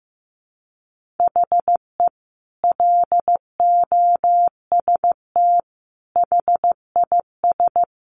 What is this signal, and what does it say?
Morse code, heard as a keyed tone, about 15 wpm, 700 Hz. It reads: HE LOST HIS
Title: Sentences from Top 1400 Words 15wpm